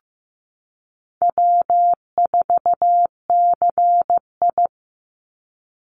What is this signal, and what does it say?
Morse code, heard as a keyed tone, about 15 wpm, 700 Hz. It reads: W4CI
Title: Easy Call Signs Repeated 2x - 15wpm